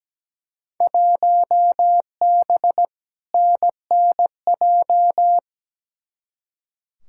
Morse code, heard as a keyed tone, about 17 wpm, 700 Hz. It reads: 1B NNJ
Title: Field Day Exchanges - 17wpm